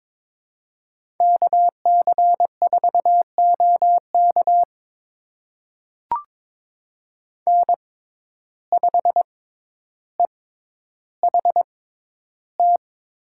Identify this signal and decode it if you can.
Morse code, heard as a keyed tone, about 22 wpm, spaced as for 11 wpm, 700 Hz. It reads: KC4OK N5EHT
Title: USA Call Signs - 6x Character Spacing - 22wpm